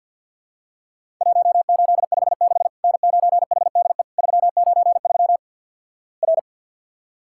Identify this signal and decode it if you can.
Morse code, heard as a keyed tone, about 48 wpm, 700 Hz. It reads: 1856 N9HDE 393 R
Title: Worked All Europe DX Contest - QTC Exchange - 50wpm